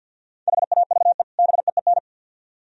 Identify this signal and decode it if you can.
Morse code, heard as a keyed tone, about 63 wpm, 700 Hz. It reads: HAVE BEEN